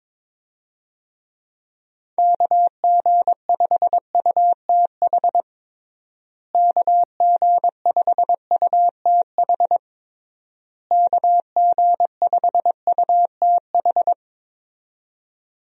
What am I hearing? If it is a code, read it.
Morse code, heard as a keyed tone, about 22 wpm, 700 Hz. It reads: KG5UTH KG5UTH KG5UTH
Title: USA Call Signs - 6 Characters - Repeated 3x - 22wpm